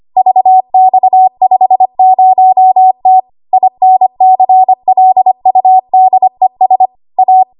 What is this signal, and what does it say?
Morse code, heard as a keyed tone, about 25 wpm, 750 Hz. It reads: VX50T INCLUDES A
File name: Archive / 25 WPM / 25WPM